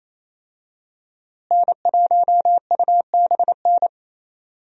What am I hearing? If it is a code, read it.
Morse code, heard as a keyed tone, about 27 wpm, 700 Hz. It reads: N1UBN
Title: USA Call Signs - 5 Characters - Repeated 2x - 28wpm